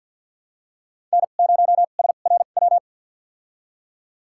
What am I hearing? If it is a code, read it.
Morse code, heard as a keyed tone, about 53 wpm, spaced as for 37 wpm, 700 Hz. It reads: N0SRW
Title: USA Call Signs - 5 Characters - 2x Character Spacing - 50wpm